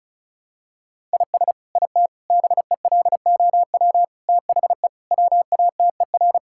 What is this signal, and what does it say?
Morse code, heard as a keyed tone, about 36 wpm, 700 Hz. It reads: IS IT BELOW THE WATER
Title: Sentences from Top 1000 Words - Review Part 5 - 35wpm